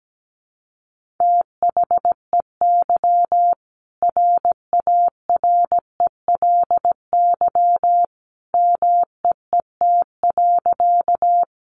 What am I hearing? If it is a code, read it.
Morse code, heard as a keyed tone, about 17 wpm, 700 Hz. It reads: THEY RARELY MEET.
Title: Sentences from Top 1400 Words 17wpm